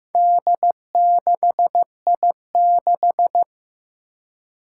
Morse code, heard as a keyed tone, about 15 wpm, 700 Hz. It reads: D6I6
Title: Four Challenging Letter-Number Combinations 15wpm